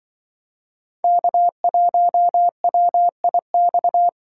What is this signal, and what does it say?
Morse code, heard as a keyed tone, about 24 wpm, 700 Hz. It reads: K1WIX
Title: Speed Racing - USA Call Signs 20wpm